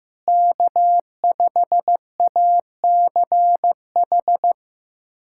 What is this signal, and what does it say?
Morse code, heard as a keyed tone, about 15 wpm, 700 Hz. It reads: K5ACH
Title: USA Callsign DE Callsign - 5 Characters - 15wpm